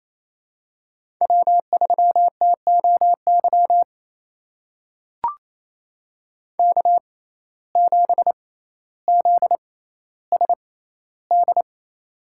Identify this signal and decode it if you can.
Morse code, heard as a keyed tone, about 28 wpm, spaced as for 14 wpm, 700 Hz. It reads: W3TOY K7ZSD
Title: USA Call Signs - 5 Characters - 6x Character Spacing - 28wpm